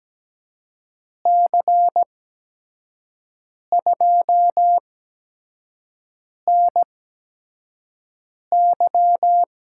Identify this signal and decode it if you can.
Morse code, heard as a keyed tone, about 17 wpm, spaced as for 5 wpm, 700 Hz. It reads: C2NY
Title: USA Call Signs - 8x Character Spacing - 17wpm